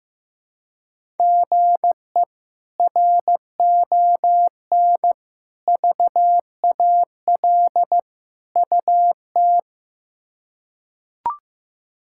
Morse code, heard as a keyed tone, about 15 wpm, 700 Hz. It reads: GE RON VAL UT
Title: K1USN SST QSO Practice - 15wpm